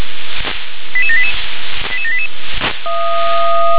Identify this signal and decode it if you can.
Five-tone selective call (ZVEI), heard as quick runs of five tones, tones of about 70 ms. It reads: 88788 88788
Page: /357/SDR/ZVEI/88747/